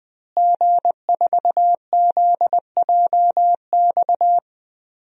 Morse code, heard as a keyed tone, about 20 wpm, 700 Hz. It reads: G4ZJX
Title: DXCC-100 Call Signs - 5 Characters - Repeated 3x - 20wpm